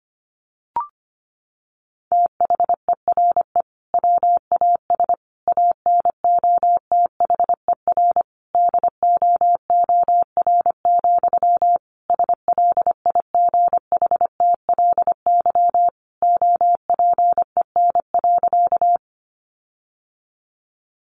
Morse code, heard as a keyed tone, about 25 wpm, 700 Hz. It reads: THERE WAS ANOTHER DOOR, SLIGHTLY OPEN.
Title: Sentences from Top 1300 Words 25wpm